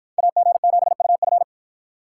Morse code, heard as a keyed tone, about 53 wpm, 700 Hz. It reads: AG7UF